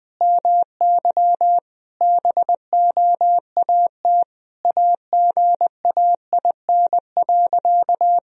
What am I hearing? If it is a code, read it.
Morse code, heard as a keyed tone, about 20 wpm, 700 Hz. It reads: MY BOAT AGAIN.
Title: Sentences from Top 1000 Words - Review Part 3 - 20wpm